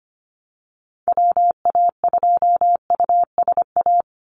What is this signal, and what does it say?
Morse code, heard as a keyed tone, about 25 wpm, 700 Hz. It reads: WA2USA
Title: NA SOTA Activator Callsigns 25wpm